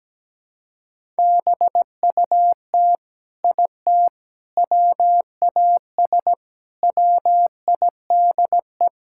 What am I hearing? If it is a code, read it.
Morse code, heard as a keyed tone, about 17 wpm, 700 Hz. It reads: BUT IT WAS WIDE
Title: Sentences from Top 1000 Words - Review Part 6 - 17wpm